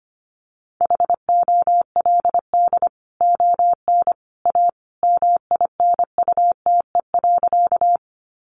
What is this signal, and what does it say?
Morse code, heard as a keyed tone, about 25 wpm, 700 Hz. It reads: HOLD ON A MINUTE.
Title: Sentences from Top 500 Words 25wpm